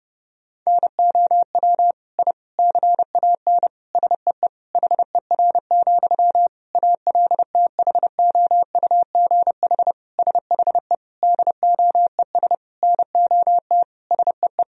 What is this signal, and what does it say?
Morse code, heard as a keyed tone, about 30 wpm, 700 Hz. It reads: NOW I CAN SEE HER, ALTHOUGH SHE DOES NOT SEE
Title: Sentences from Top 1000 Words - Review Part 3 - 30wpm